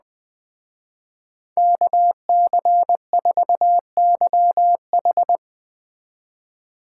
Morse code, heard as a keyed tone, about 21 wpm, 700 Hz. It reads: KC4YH
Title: Speed Racing - USA 2x2 Call Signs 17wpm